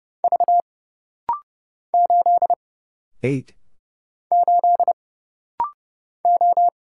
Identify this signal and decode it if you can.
Morse code, heard as a keyed tone, about 31 wpm, 700 Hz. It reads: V 8 8 O
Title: Single Letter-Number Alt - Rapid-Fire 30wpm